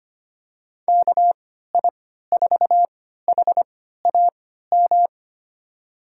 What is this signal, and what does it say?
Morse code, heard as a keyed tone, about 25 wpm, spaced as for 14 wpm, 700 Hz. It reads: KI4HAM
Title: USA Call Signs - 6 Characters - 3x Character Spacing - 25wpm